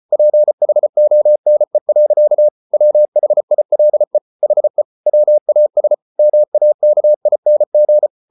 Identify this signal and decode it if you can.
Morse code, heard as a keyed tone, about 34 wpm, 600 Hz. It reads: PHONE. WHILE HE WAS MAKING